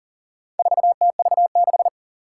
Morse code, heard as a keyed tone, about 40 wpm, 700 Hz. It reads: 4TV6
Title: Four Challenging Letter-Number Combinations 40wpm